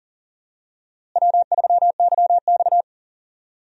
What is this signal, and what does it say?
Morse code, heard as a keyed tone, about 40 wpm, 700 Hz. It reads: W3YX
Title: USA Call Signs - 4 Characters - Repeated 3x - 40wpm